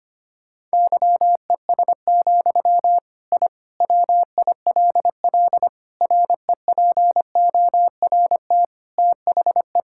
Morse code, heard as a keyed tone, about 25 wpm, 700 Hz. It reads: YES, I WILL REPORT THE
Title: Sentences from Top 1000 Words - Review Part 6 - 25wpm